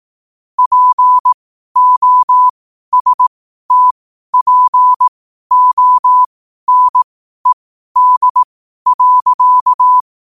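Morse code, heard as a keyed tone, about 18 wpm, spaced as for 12 wpm, 1000 Hz. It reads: POSTPONED.
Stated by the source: Morse code at 12 wpm.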